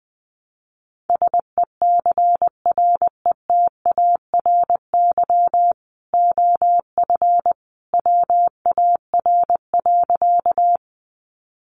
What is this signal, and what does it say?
Morse code, heard as a keyed tone, about 20 wpm, 700 Hz. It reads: SECRETARY OF WAR.
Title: Sentences from Top 1300 Words 20wpm